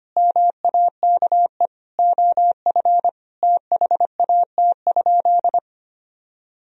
Morse code, heard as a keyed tone, about 25 wpm, 700 Hz. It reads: MAKE OF THAT?
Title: Sentences from Top 100 Words and Abbreviations 25wpm